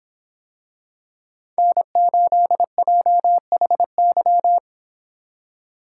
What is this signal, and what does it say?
Morse code, heard as a keyed tone, about 25 wpm, 700 Hz. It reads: N8JHY